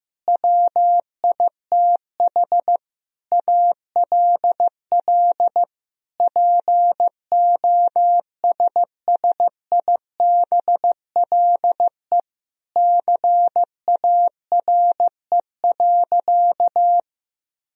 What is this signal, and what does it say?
Morse code, heard as a keyed tone, about 15 wpm, 700 Hz. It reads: WITH ALL POSSIBLE CARE.
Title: Sentences from Top 1000 Words - Review Part 2 - 15wpm